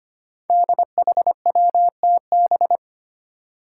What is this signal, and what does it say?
Morse code, heard as a keyed tone, about 25 wpm, 700 Hz. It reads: DHWTB
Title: Five Challenging Letter-Number Combinations 25wpm